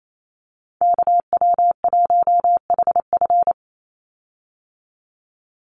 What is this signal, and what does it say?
Morse code, heard as a keyed tone, about 28 wpm, 700 Hz. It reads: KW1HF
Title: USA Call Signs 28wpm